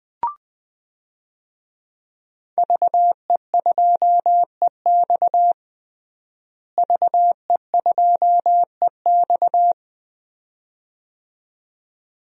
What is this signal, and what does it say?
Morse code, heard as a keyed tone, about 20 wpm, 700 Hz. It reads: VE2EX VE2EX
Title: DXCC-100 Call Signs - 5 Characters - Repeated 2x - 20wpm